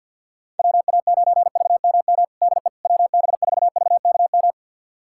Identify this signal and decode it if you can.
Morse code, heard as a keyed tone, about 47 wpm, 700 Hz. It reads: WA9VMM DE WB4VKM
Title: USA Callsign DE Callsign - 6 Characters - 50wpm